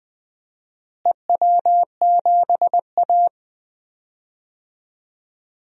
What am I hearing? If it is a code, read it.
Morse code, heard as a keyed tone, about 20 wpm, 700 Hz. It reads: EW7A
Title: DXCC-100 Call Signs - 4 Characters - Repeated 2x - 20wpm